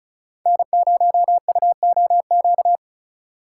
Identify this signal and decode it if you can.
Morse code, heard as a keyed tone, about 36 wpm, 700 Hz. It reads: N0UOQ
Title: USA Call Signs - 5 Characters - 2x Character Spacing - 35wpm